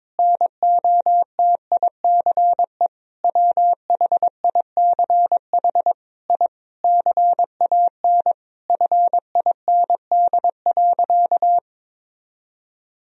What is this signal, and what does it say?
Morse code, heard as a keyed tone, about 22 wpm, 700 Hz. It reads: NOTICE WHICH I CAN FIND.
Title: Sentences from Top 1000 Words - Review Part 2 - 22wpm